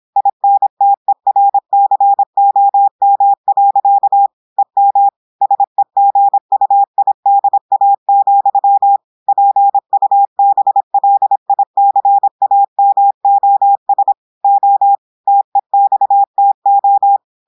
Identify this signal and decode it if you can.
Morse code, heard as a keyed tone, about 26 wpm, 800 Hz. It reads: INTERCOM. EM SEGUIDA, PUBLICAMOS O TEXTO
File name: audios / portugais / portugais texte en clair 1 vitesse 26 mots minutes